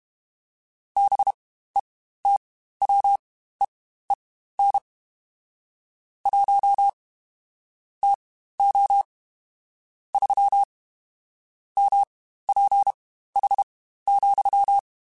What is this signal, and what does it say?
Morse code, heard as a keyed tone, about 32 wpm, spaced as for 15 wpm, 800 Hz. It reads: BETWEEN 1 TO 3 MPH,
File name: Spaced Characters / Spaced Characters-mp3s / Weather 5 5 sp @30